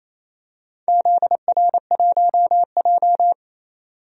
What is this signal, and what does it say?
Morse code, heard as a keyed tone, about 28 wpm, 700 Hz. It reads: ZR1J
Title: DXCC-100 Call Signs - 4 Characters - 28wpm